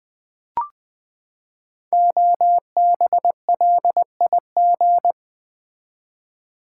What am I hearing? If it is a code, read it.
Morse code, heard as a keyed tone, about 20 wpm, 700 Hz. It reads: OBLIG
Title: Build A Word - 5 Letters At A Time - 10-Letter Words - 20wpm